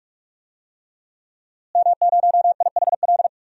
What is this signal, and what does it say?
Morse code, heard as a keyed tone, about 44 wpm, 700 Hz. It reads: M0IHL
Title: DXCC-100 Call Signs - 5 Characters - Repeated 3x - 45wpm